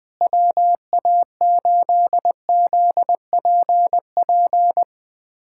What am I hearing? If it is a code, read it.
Morse code, heard as a keyed tone, about 20 wpm, 700 Hz. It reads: WA8ZPP
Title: USA Callsign DE Callsign - 6 Characters - 20wpm